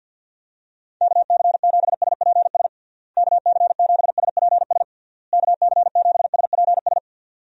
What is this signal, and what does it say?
Morse code, heard as a keyed tone, about 48 wpm, 700 Hz. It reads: KC7SPS KC7SPS KC7SPS
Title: USA Call Signs - 6 Characters - Repeated 3x - 50wpm